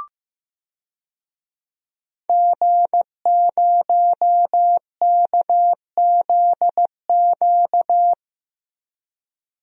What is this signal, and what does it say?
Morse code, heard as a keyed tone, about 15 wpm, 700 Hz. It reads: G0KZQ
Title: DXCC-100 Call Signs - 5 Characters - 15wpm